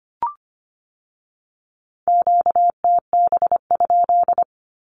Morse code, heard as a keyed tone, about 25 wpm, 700 Hz. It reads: QTB?
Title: Q-Codes with Spoken Meaning - 25wpm